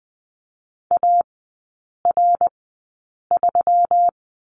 Morse code, heard as a keyed tone, about 20 wpm, spaced as for 8 wpm, 700 Hz. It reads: AR3
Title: 121 Lesson 20e - Number 3 - Three Characters 20wpm